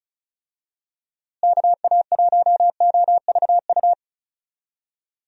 Morse code, heard as a keyed tone, about 34 wpm, 700 Hz. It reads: KA1OVU
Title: USA Call Signs - 6 Characters - Repeated 3x - 35wpm